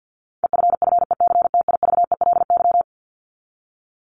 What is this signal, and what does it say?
Morse code, heard as a keyed tone, about 50 wpm, 700 Hz. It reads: EFFECTIVELY
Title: Top 1251-1500 Words 50wpm